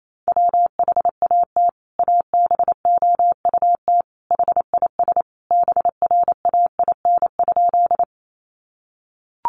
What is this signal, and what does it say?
Morse code, heard as a keyed tone, about 28 wpm, 700 Hz. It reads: WHAT ABOUT HIS BRAIN?
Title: Sentences from Top 1100 Words 28wpm